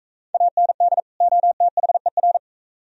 Morse code, heard as a keyed tone, about 44 wpm, 700 Hz. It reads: AND OTHER